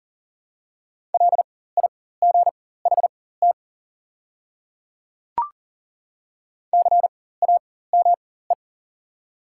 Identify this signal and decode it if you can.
Morse code, heard as a keyed tone, about 41 wpm, spaced as for 19 wpm, 700 Hz. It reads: LIGHT CAME
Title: 4x Character Spacing - Top 200 Words - 40wpm